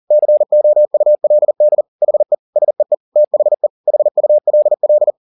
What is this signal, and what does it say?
Morse code, heard as a keyed tone, about 40 wpm, 600 Hz. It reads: COULD HE SEE THE HULL
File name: the old man and the sea 2 40wpm